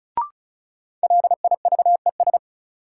Morse code, heard as a keyed tone, about 36 wpm, 700 Hz. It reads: LIVES
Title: Sets of 5 Words Encore - New Word Drill - 35wpm